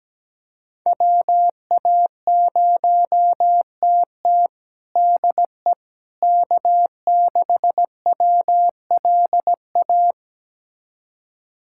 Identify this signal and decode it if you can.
Morse code, heard as a keyed tone, about 17 wpm, 700 Hz. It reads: WA0TT DE K6WLA
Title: USA Callsign DE Callsign - 5 Characters - 17wpm